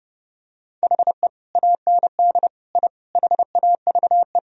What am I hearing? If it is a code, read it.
Morse code, heard as a keyed tone, about 30 wpm, 700 Hz. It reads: HE AND I HAVE
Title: Sentences from Top 1000 Words - Review Part 3 - 30wpm